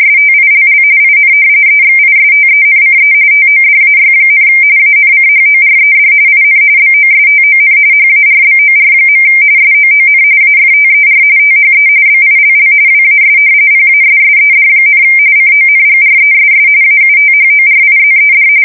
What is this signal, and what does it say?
RTTY, about 45.45 baud. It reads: STILL IN EXISTENCE TODAY, ANCHORED IN BOSTON HARBOR. SOON AFTER THE BILL WAS PASSED, CONGRESS AUTHORIZED
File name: fsk170 / usnavy1